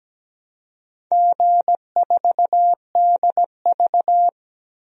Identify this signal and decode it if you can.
Morse code, heard as a keyed tone, about 17 wpm, 700 Hz. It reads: G4DV
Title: DXCC-100 Call Signs - 4 Characters - 17wpm